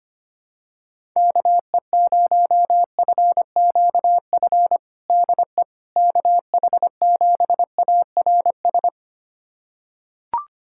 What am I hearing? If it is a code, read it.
Morse code, heard as a keyed tone, about 25 wpm, 700 Hz. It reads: KE0FQF DE KH7ARS
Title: USA Callsign DE Callsign - 6 Characters - 25wpm